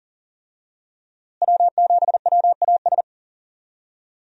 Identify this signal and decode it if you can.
Morse code, heard as a keyed tone, about 40 wpm, 700 Hz. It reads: W7WAS
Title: USA Call Signs - 5 Characters - Repeated 3x - 40wpm